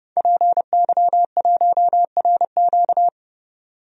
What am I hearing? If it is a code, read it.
Morse code, heard as a keyed tone, about 31 wpm, 700 Hz. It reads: PY1RQ
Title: DXCC-100 Call Signs - 5 Characters - 30wpm